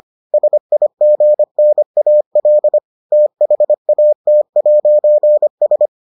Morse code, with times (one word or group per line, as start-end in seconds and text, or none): 0.3-2.8 SIGNAL
3.1-5.9 THAT'S